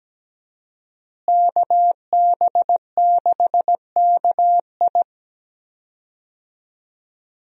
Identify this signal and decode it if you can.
Morse code, heard as a keyed tone, about 17 wpm, 700 Hz. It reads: KB6KI
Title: USA Call Signs - 5 Characters - Repeated 3x - 17wpm